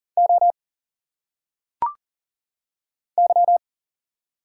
Morse code, heard as a keyed tone, about 40 wpm, 700 Hz. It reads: O Y
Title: Single Letters - Phonetic Pronunciation 40wpm